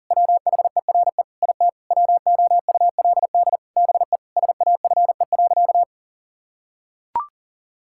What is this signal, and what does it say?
Morse code, heard as a keyed tone, about 40 wpm, 700 Hz. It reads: WHERE IT WOULD BE SAFE.
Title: Sentences from Top 1600 Words 40wpm